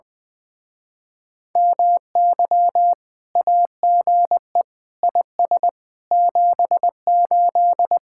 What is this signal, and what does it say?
Morse code, heard as a keyed tone, about 20 wpm, 700 Hz. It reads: MY AGE IS 78
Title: Speed-Racing Common QSO Phrases - 20wpm